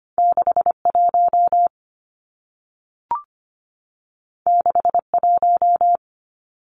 Morse code, heard as a keyed tone, about 25 wpm, 700 Hz. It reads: 61 61
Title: Numbers - 2-Digit 25wpm